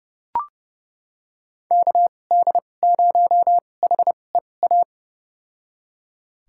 Morse code, heard as a keyed tone, about 31 wpm, spaced as for 22 wpm, 700 Hz. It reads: KD0HEA
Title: USA Call Signs - 6 Characters - 2x Character Spacing - 30wpm